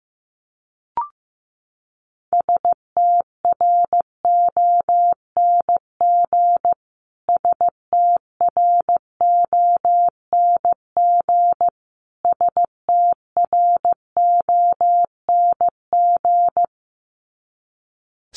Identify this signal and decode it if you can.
Morse code, heard as a keyed tone, about 15 wpm, 700 Hz. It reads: STRONG STRONG STRONG
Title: Top 500 Words Repeated 15wpm